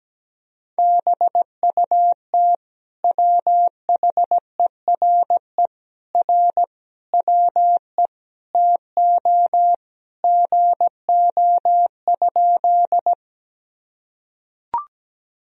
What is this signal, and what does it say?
Morse code, heard as a keyed tone, about 17 wpm, 700 Hz. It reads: BUT WHERE R WE TO GO?
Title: Sentences from Top 200 Words and Abbreviations 17wpm